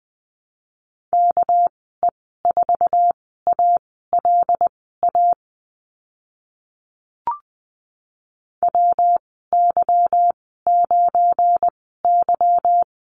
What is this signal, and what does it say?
Morse code, heard as a keyed tone, about 20 wpm, spaced as for 14 wpm, 700 Hz. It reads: KE4ALA WY9Y
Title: 200 Lesson 32g - Letter k - Call Signs 20wpm